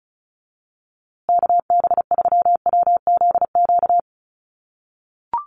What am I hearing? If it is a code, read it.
Morse code, heard as a keyed tone, about 34 wpm, 700 Hz. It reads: KB3WZQ
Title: USA Call Signs Repeated 3x 35wpm